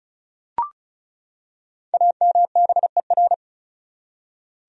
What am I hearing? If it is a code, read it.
Morse code, heard as a keyed tone, about 36 wpm, 700 Hz. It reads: AMBER
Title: USA Common Names - 35wpm